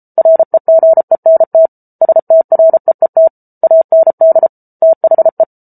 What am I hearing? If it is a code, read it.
Morse code, heard as a keyed tone, about 34 wpm, 650 Hz. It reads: REGENT STREET AND THE